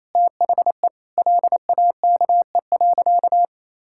Morse code, heard as a keyed tone, about 28 wpm, 700 Hz. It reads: THE LAKE.
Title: Sentences from Top 1600 Words 28wpm